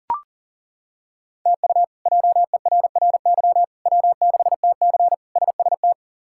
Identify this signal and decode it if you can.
Morse code, heard as a keyed tone, about 40 wpm, 700 Hz. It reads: TU JERRY W6TC SST
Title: K1USN SST QSO Practice - 40wpm